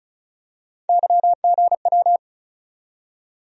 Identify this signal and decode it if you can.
Morse code, heard as a keyed tone, about 34 wpm, 700 Hz. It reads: YGW
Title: USA Call Sign Postfixes - 3 Letters 35wpm